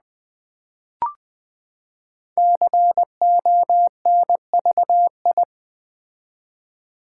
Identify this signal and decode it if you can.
Morse code, heard as a keyed tone, about 20 wpm, 700 Hz. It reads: CONVI
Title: Build A Word - 5 Letters At A Time - 10-Letter Words - 20wpm